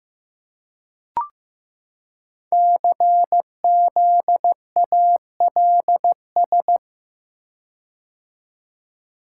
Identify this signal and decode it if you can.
Morse code, heard as a keyed tone, about 15 wpm, 700 Hz. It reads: CZALS
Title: Code Groups - 5 Letters - 15wpm